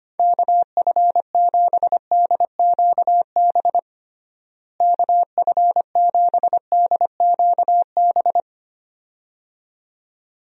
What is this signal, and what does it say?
Morse code, heard as a keyed tone, about 25 wpm, 700 Hz. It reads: KF7DQB KF7DQB
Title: USA Call Signs - 6 Characters - Repeated 2x - 25wpm